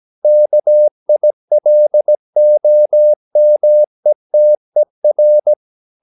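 Morse code, heard as a keyed tone, about 17 wpm, 600 Hz. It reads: KILOMETER
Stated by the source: Slow (QRS) Morse practice at 17 wpm, from CW med Gnister.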